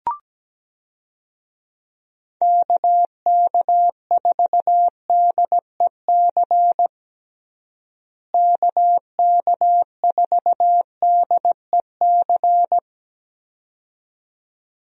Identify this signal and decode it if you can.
Morse code, heard as a keyed tone, about 17 wpm, 700 Hz. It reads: KK4DEC KK4DEC
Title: USA Call Signs - 6 Characters - Repeated 2x - 17wpm